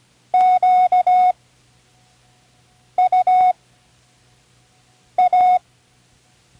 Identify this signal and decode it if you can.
Morse code, heard as a keyed tone, about 17 wpm, spaced as for 5 wpm, 730 Hz. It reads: QUA
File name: Learning Morse Code (from Tune in the World)